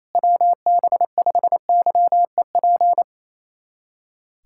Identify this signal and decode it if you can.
Morse code, heard as a keyed tone, about 28 wpm, 700 Hz. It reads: WB5YEP
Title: USA Callsign DE Callsign - 6 Characters - 28wpm